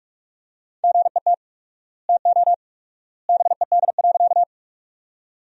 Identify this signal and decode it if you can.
Morse code, heard as a keyed tone, about 47 wpm, 700 Hz. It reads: GET TO BED.
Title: 4x Word-Spacing - Sentences from Top 500 Words - 45wpm